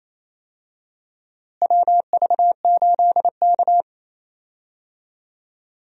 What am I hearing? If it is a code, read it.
Morse code, heard as a keyed tone, about 28 wpm, 700 Hz. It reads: WV8K
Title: USA Call Signs - 4 Characters - Repeated 3x - 28wpm